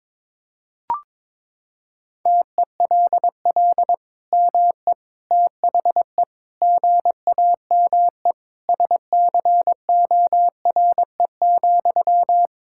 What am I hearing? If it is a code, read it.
Morse code, heard as a keyed tone, about 22 wpm, 700 Hz. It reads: TELL ME THE GAME SCORE,
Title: Sentences from Top 1600 Words 22wpm